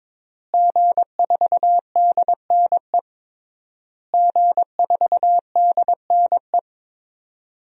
Morse code, heard as a keyed tone, about 22 wpm, 700 Hz. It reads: G4DNE G4DNE
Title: DXCC-100 Call Signs - 5 Characters - Repeated 3x - 22wpm